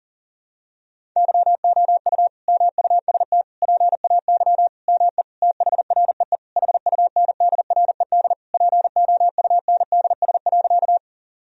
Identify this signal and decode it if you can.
Morse code, heard as a keyed tone, about 40 wpm, 700 Hz. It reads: YOU MUST PAY ME THREE HUNDRED POUNDS.
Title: Sentences from Top 1000 Words - Review Part 2 - 40wpm